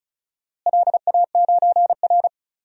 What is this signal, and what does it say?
Morse code, heard as a keyed tone, about 36 wpm, 700 Hz. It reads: LA9R
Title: DXCC-100 Call Signs - 4 Characters - Repeated 2x - 35wpm